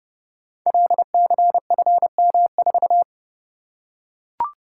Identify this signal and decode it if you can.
Morse code, heard as a keyed tone, about 28 wpm, 700 Hz. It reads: LCFM4
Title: Code Groups - 5 Letter:Numbers - 30wpm